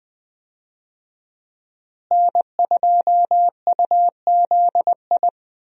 Morse code, heard as a keyed tone, about 20 wpm, 700 Hz. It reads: N2UZI
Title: USA Call Signs - 5 Characters - Repeated 2x - 20wpm